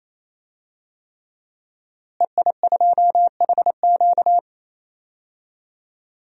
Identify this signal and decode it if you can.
Morse code, heard as a keyed tone, about 28 wpm, 700 Hz. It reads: EI2HQ
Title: DXCC-100 Call Signs - 5 Characters - Repeated 3x - 28wpm